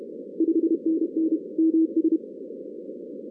Morse code, heard as a keyed tone, about 31 wpm, 320 Hz. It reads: 5NN MS
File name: 2016 arrl dx cw / 2016-ARRL-DX-CW-K5TR- - 0-46